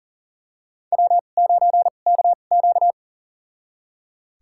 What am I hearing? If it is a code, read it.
Morse code, heard as a keyed tone, about 40 wpm, spaced as for 29 wpm, 700 Hz. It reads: W9KQ
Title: USA Call Signs - 2x Character Spacing - 40wpm